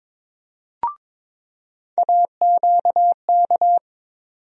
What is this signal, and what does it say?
Morse code, heard as a keyed tone, about 22 wpm, 700 Hz. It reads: AQK
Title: USA Call Sign Postfixes - 3 Letters 22wpm